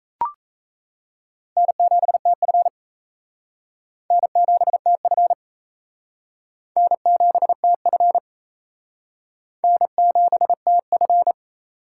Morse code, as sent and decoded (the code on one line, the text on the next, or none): -. --... - ..-. / -. --... - ..-. / -. --... - ..-. / -. --... - ..-.
N7TF N7TF N7TF N7TF